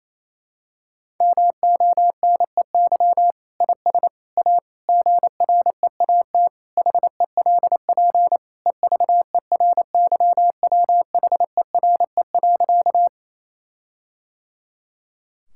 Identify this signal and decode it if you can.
Morse code, heard as a keyed tone, about 28 wpm, 700 Hz. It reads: MONEY IS A GREAT HELP EVERYWHERE.
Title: Sentences from Top 1600 Words 28wpm